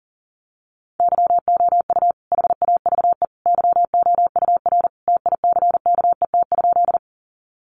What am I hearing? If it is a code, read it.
Morse code, heard as a keyed tone, about 40 wpm, 700 Hz. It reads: YOU HAVE YOUR TICKET?
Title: Sentences from Top 1400 Words 40wpm